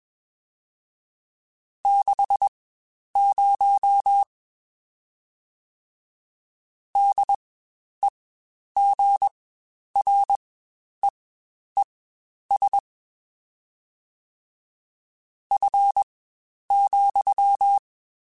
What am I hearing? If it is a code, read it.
Morse code, heard as a keyed tone, about 21 wpm, spaced as for 10 wpm, 800 Hz. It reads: 60 DEGREES F,
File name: Spaced Characters / Spaced Characters-mp3s / Weather5 22 sp @20